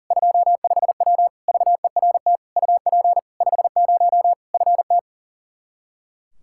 Morse code, heard as a keyed tone, about 40 wpm, 700 Hz. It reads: 25W VERT UP 50 FT
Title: Practice QSO - Exchange 2 - Info - 40wpm